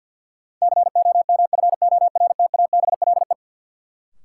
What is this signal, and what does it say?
Morse code, heard as a keyed tone, about 53 wpm, 700 Hz. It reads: COMFORTABLE
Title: Sentences from Top 1000 Words - Review Part 7 - New Word Drill - 50wpm